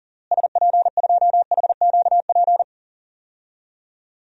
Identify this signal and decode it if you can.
Morse code, heard as a keyed tone, about 40 wpm, 700 Hz. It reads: SP2HQP
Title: DXCC-100 Call Signs - 6 Characters - Repeated 3x - 40wpm